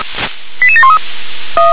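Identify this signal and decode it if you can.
Five-tone selective call (ZVEI), heard as quick runs of five tones, tones of about 70 ms. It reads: 88713
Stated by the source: ZVEI five-tone selective call received by SDR.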